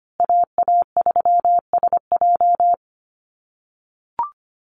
Morse code, heard as a keyed tone, about 25 wpm, 700 Hz. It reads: AA3SJ
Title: USA Call Signs Repeated 3x 25wpm